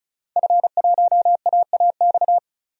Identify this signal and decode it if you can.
Morse code, heard as a keyed tone, about 34 wpm, 700 Hz. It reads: F1AAX
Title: DXCC-100 Call Signs - 5 Characters - 35wpm